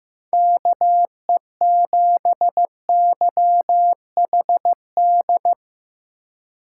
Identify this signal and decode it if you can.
Morse code, heard as a keyed tone, about 16 wpm, 700 Hz. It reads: KE7YHD
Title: Speed Racing - USA 2x3 Call Signs 15wpm